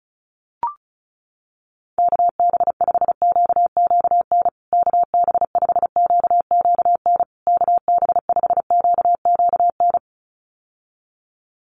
Morse code, heard as a keyed tone, about 34 wpm, 700 Hz. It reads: KB5QQN KB5QQN KB5QQN
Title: USA Call Signs Repeated 3x 35wpm